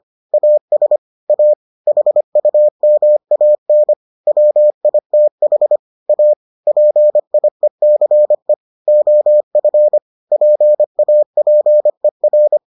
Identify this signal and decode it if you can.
Morse code, heard as a keyed tone, about 25 wpm, 600 Hz. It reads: AS A HUMAN WITH A PIECE OF PAPER